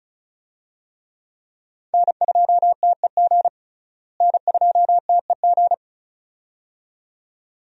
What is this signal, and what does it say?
Morse code, heard as a keyed tone, about 34 wpm, 700 Hz. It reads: N2TEG N2TEG
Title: USA Call Signs - 5 Characters - Repeated 2x - 35wpm